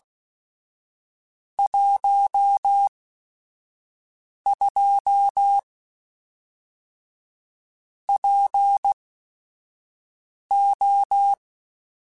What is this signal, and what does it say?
Morse code, heard as a keyed tone, about 16 wpm, spaced as for 5 wpm, 800 Hz. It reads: 12 PO
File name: Technical Subjects / Technical Subjects-mp3s / AND Gate @5